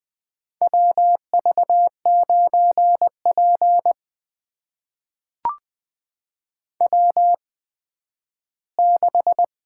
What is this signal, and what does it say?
Morse code, heard as a keyed tone, about 20 wpm, spaced as for 12 wpm, 700 Hz. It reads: WV9P W6
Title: USA Call Signs - 4 Characters - 8x Character Spacing - 20wpm